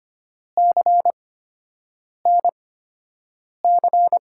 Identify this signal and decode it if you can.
Morse code, heard as a keyed tone, about 25 wpm, spaced as for 7 wpm, 700 Hz. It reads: CNC
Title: USA Call Signs - 5 Characters - 8x Character Spacing - 25wpm